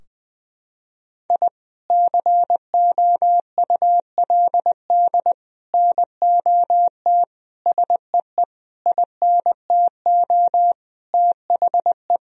Morse code, heard as a keyed tone, about 21 wpm, 700 Hz. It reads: I COULD NOT SEE INTO THE
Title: Sentences from Top 1000 Words - Review Part 4 - 20wpm